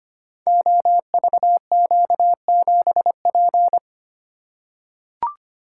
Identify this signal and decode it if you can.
Morse code, heard as a keyed tone, about 25 wpm, 700 Hz. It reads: OVQ7P
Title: Code Groups - 5 Letter:Numbers - 25wpm